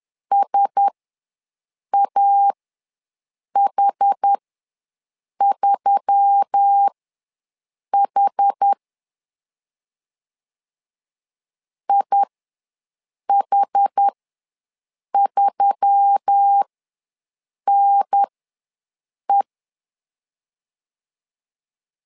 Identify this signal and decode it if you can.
Morse code, heard as a keyed tone, about 10 wpm, spaced as for 6 wpm, 800 Hz. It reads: SAH3H IH3NE
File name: LEZIONI / 4 Q1S3 ws 10wpm 2437